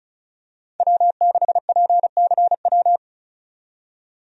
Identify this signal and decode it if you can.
Morse code, heard as a keyed tone, about 34 wpm, 700 Hz. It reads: W6PCW
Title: USA Call Signs - 5 Characters - 35wpm